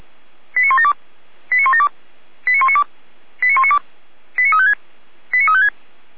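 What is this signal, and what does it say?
Five-tone selective call (ZVEI), heard as quick runs of five tones, tones of about 70 ms. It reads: 89172 89172 89192 89192 89356 89356
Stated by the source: ZVEI five-tone selective call received on SDR.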